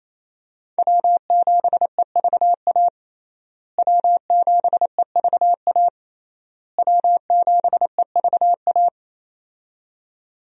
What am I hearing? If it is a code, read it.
Morse code, heard as a keyed tone, about 28 wpm, 700 Hz. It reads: W7EVA W7EVA W7EVA
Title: USA Call Signs - 5 Characters - Repeated 3x - 28wpm